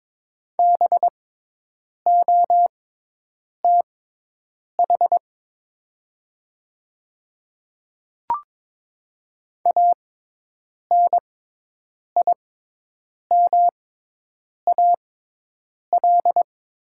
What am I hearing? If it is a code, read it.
Morse code, heard as a keyed tone, about 22 wpm, spaced as for 8 wpm, 700 Hz. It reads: BOTH ANIMAL
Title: 6x Character Spacing - Top 300 Words - 22wpm